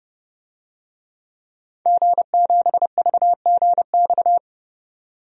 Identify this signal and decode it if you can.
Morse code, heard as a keyed tone, about 31 wpm, 700 Hz. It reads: G7VGX
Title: DXCC-100 Call Signs - 5 Characters - 30wpm